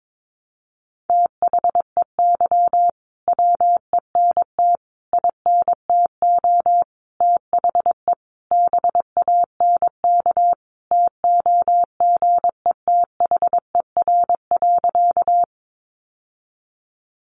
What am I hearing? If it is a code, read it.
Morse code, heard as a keyed tone, about 22 wpm, 700 Hz. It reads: THEY WENT INTO THE BANK TOGETHER.
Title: Sentences from Top 900 Words 22wpm